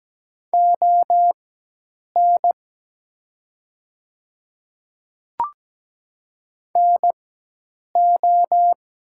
Morse code, heard as a keyed tone, about 17 wpm, spaced as for 8 wpm, 700 Hz. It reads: ON NO
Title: 4x Character Spacing - Top 100 Words - 17wpm